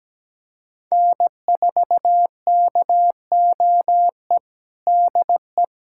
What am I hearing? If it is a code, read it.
Morse code, heard as a keyed tone, about 17 wpm, 700 Hz. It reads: N4KOE DE
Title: USA Callsign DE Callsign - 5 Characters - 17wpm